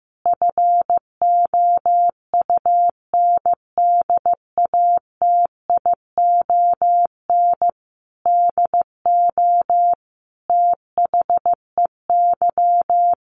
Morse code, heard as a keyed tone, about 15 wpm, 700 Hz. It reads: FOUNDATION DO THEY
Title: Sentences from Top 1200 Words 15wpm